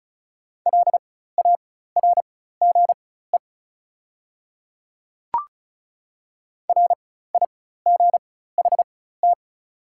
Morse code, heard as a keyed tone, about 36 wpm, spaced as for 16 wpm, 700 Hz. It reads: LARGE RIGHT
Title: 4x Character Spacing - Top 200 Words - 35wpm